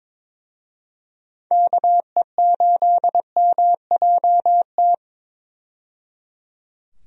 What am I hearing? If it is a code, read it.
Morse code, heard as a keyed tone, about 22 wpm, 700 Hz. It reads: KE8MJT
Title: USA Call Signs - 6 Characters - 22wpm